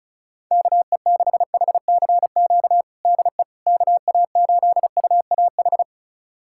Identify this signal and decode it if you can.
Morse code, heard as a keyed tone, about 36 wpm, 700 Hz. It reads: KE6HCQ DE KA8UAH
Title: USA Callsign DE Callsign 35wpm